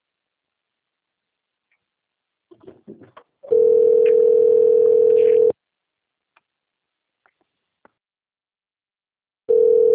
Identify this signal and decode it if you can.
Ringback tone (North America), 440 + 480 Hz.